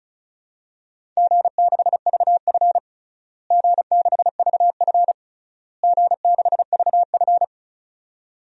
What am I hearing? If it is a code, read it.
Morse code, heard as a keyed tone, about 36 wpm, 700 Hz. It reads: G6VF G6VF G6VF
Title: DXCC-100 Call Signs - 4 Characters - Repeated 3x - 35wpm